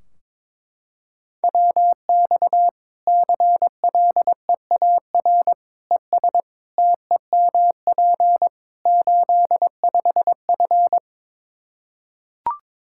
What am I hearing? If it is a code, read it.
Morse code, heard as a keyed tone, about 22 wpm, 700 Hz. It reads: WX CLEAR ES TEMP 85F
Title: Practice QSO - Exchange 2 - Info - 22wpm